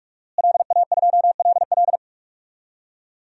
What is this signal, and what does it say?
Morse code, heard as a keyed tone, about 41 wpm, 700 Hz. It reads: LA2LL
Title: DXCC-100 Call Signs - 5 Characters - Repeated 2x - 45wpm